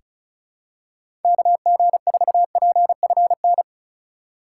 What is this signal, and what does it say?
Morse code, heard as a keyed tone, about 36 wpm, 700 Hz. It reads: KG4PFN
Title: USA Call Signs - 6 Characters - 3x Character Spacing - 35wpm